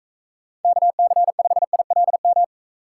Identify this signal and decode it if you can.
Morse code, heard as a keyed tone, about 40 wpm, 700 Hz. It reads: KC5ILM